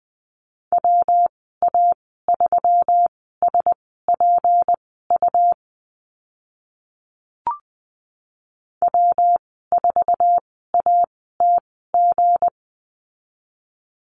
Morse code, heard as a keyed tone, about 20 wpm, spaced as for 14 wpm, 700 Hz. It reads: WA3SPU W4ATG
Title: 238 Lesson 38g - Letter q - Call Signs 20wpm